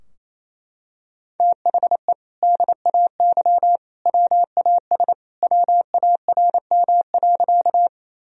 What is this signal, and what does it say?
Morse code, heard as a keyed tone, about 28 wpm, 700 Hz. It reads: THE DAY WAS WARM.
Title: Sentences from Top 1000 Words - Review Part 2 - 28wpm